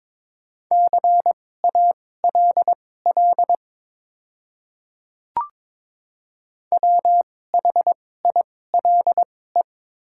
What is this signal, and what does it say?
Morse code, heard as a keyed tone, about 22 wpm, spaced as for 16 wpm, 700 Hz. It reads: CALL WHILE
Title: 2x Character Spacing - Top 300 Words - 22wpm